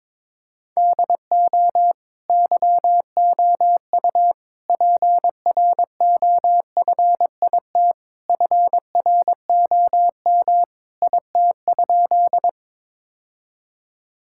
Morse code, heard as a keyed tone, about 22 wpm, 700 Hz. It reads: DO YOU PROFIT FROM IT?
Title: Sentences from Top 1000 Words - Review Part 5 - 22wpm